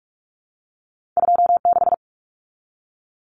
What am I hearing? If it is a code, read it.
Morse code, heard as a keyed tone, about 46 wpm, 700 Hz. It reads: 26
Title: Common Cut Numbers - 2-Digit 45wpm